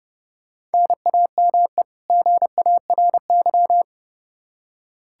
Morse code, heard as a keyed tone, about 30 wpm, 700 Hz. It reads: NAME GARY
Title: Practice QSO - Exchange 1 - Beginning - 30wpm